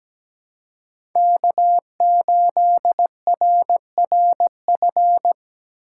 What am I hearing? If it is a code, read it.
Morse code, heard as a keyed tone, about 17 wpm, 700 Hz. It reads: K8RRF
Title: USA Call Signs - 5 Characters - 6x Character Spacing - 17wpm